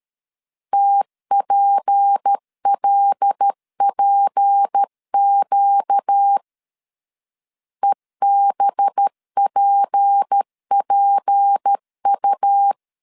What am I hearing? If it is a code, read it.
Morse code, heard as a keyed tone, about 13 wpm, 800 Hz. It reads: TPLPQ EBPPU
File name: LEZIONI / 6 PKX0 / 6 P 12wpm 2033